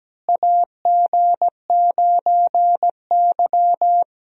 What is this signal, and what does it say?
Morse code, heard as a keyed tone, about 17 wpm, 700 Hz. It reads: AG9Y
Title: USA Call Signs - 4 Characters - Repeated 2x - 17wpm